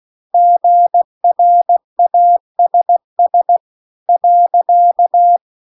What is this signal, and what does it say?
Morse code, heard as a keyed tone, about 16 wpm, 700 Hz. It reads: GRASS .
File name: 16WPM